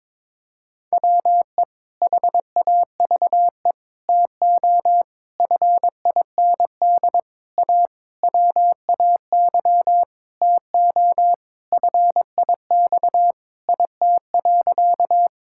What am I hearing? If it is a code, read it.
Morse code, heard as a keyed tone, about 22 wpm, 700 Hz. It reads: WE HAVE TO FIND A WAY TO FIX IT.
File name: practice / 22wpm / Sentences with 7-Word Phrases - 22wpm